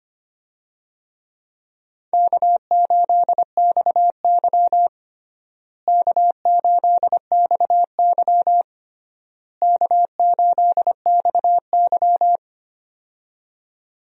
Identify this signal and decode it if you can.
Morse code, heard as a keyed tone, about 25 wpm, 700 Hz. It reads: K8XY K8XY K8XY
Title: USA Call Signs - 4 Characters - Repeated 3x - 25wpm